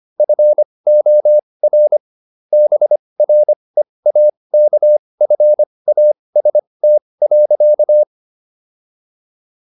Morse code, heard as a keyed tone, about 25 wpm, spaced as for 20 wpm, 600 Hz. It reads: FOR BREAKFAST.